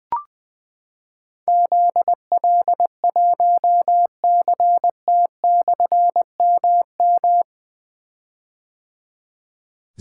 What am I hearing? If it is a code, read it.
Morse code, heard as a keyed tone, about 20 wpm, 700 Hz. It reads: ZL1CT/MM
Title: Difficult Call Signs 20wpm